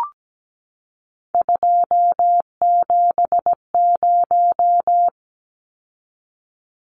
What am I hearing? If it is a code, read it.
Morse code, heard as a keyed tone, about 17 wpm, 700 Hz. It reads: 270
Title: Numbers - 3-Digit 17wpm